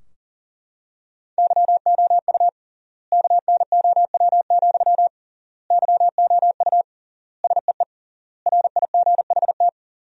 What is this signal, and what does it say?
Morse code, heard as a keyed tone, about 40 wpm, 700 Hz. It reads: YOU KNOW, YOU SEE RIGHT
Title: 3x Word-Spacing - Sentences from Top 200 Words - 40wpm